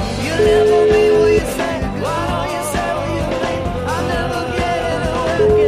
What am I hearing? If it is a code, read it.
Ringback tone (Europe, 425 Hz), with noise.